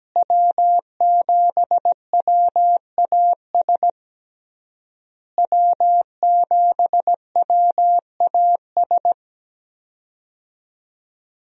Morse code, heard as a keyed tone, about 17 wpm, 700 Hz. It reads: W7WAS W7WAS
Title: USA Call Signs - 5 Characters - Repeated 3x - 17wpm